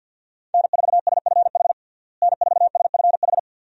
Morse code, heard as a keyed tone, about 52 wpm, 700 Hz. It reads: N4SFH N4SFH
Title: USA Call Signs - 5 Characters - Repeated 2x - 50wpm